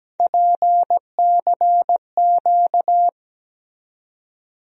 Morse code, heard as a keyed tone, about 17 wpm, 700 Hz. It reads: PCQ
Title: USA Call Sign Postfixes - 3 Letters 17wpm